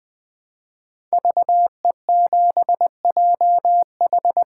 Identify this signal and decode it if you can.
Morse code, heard as a keyed tone, about 20 wpm, 700 Hz. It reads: VE7JH
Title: 2023 SOTA Chaser Call Signs - 20wpm